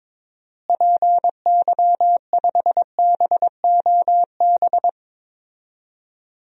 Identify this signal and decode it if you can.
Morse code, heard as a keyed tone, about 22 wpm, 700 Hz. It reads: PY5BOB
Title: DXCC-100 Call Signs - 6 Characters - Repeated 2x - 22wpm